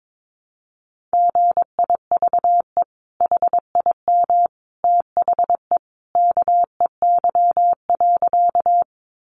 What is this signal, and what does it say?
Morse code, heard as a keyed tone, about 22 wpm, 700 Hz. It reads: GIVE HIM THE KEY.
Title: Sentences from Top 600 Words 22wpm